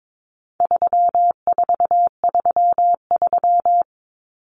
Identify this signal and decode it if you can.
Morse code, heard as a keyed tone, about 22 wpm, 700 Hz. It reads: 3433
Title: Numbers - 4-Digit 22wpm